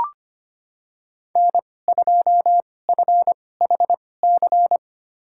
Morse code, heard as a keyed tone, about 25 wpm, spaced as for 18 wpm, 700 Hz. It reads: N2FHC
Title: USA Call Signs - 5 Characters - 2x Character Spacing - 25wpm